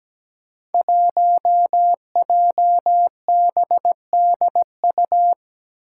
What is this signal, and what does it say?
Morse code, heard as a keyed tone, about 17 wpm, 700 Hz. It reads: 1JBDU
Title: Five Challenging Letter-Number Combinations 17wpm